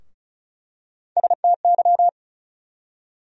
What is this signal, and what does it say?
Morse code, heard as a keyed tone, about 36 wpm, 700 Hz. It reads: STY
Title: USA Call Sign Postfixes - 3 Letters 35wpm